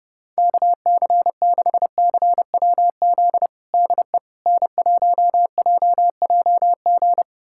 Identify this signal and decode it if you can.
Morse code, heard as a keyed tone, about 30 wpm, 700 Hz. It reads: KC6CWZ DE N1JJG
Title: USA Callsign DE Callsign 30wpm